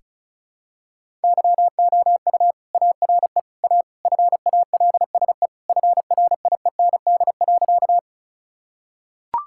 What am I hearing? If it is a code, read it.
Morse code, heard as a keyed tone, about 36 wpm, 700 Hz. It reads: YOU ARE A FALSE FRIEND.
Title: Sentences from Top 1600 Words 35wpm